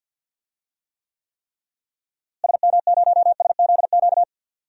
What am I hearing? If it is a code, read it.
Morse code, heard as a keyed tone, about 47 wpm, 700 Hz. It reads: SM0SZQ
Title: DXCC-100 Call Signs - 6 Characters - Repeated 3x - 50wpm